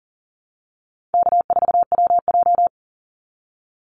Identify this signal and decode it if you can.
Morse code, heard as a keyed tone, about 38 wpm, 700 Hz. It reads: K4WJ
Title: Call Signs Repeated 2x - 40wpm